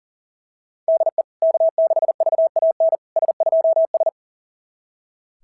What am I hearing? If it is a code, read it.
Morse code, heard as a keyed tone, about 40 wpm, 650 Hz. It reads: DE K6VAN S2S
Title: SOTA QSO Practice 40wpm